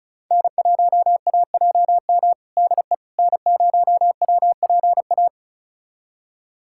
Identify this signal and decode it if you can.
Morse code, heard as a keyed tone, about 36 wpm, 700 Hz. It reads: N1AJM DE N0WPA
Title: USA Callsign DE Callsign 35wpm